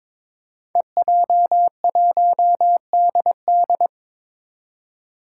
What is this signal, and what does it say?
Morse code, heard as a keyed tone, about 22 wpm, 700 Hz. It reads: EJ1DD
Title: Five Challenging Letter-Number Combinations 22wpm